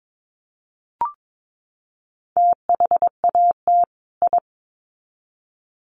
Morse code, heard as a keyed tone, about 22 wpm, 700 Hz. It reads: THAT I
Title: Sets of 2 Words 22wpm